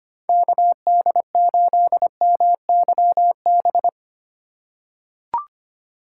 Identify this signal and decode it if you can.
Morse code, heard as a keyed tone, about 25 wpm, 700 Hz. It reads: KD8MYB
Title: USA Call Signs - 3x Character Spacing - 25wpm